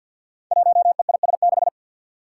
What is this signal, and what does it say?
Morse code, heard as a keyed tone, about 47 wpm, 700 Hz. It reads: 1EIS6
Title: Five Challenging Letter-Number Combinations 50wpm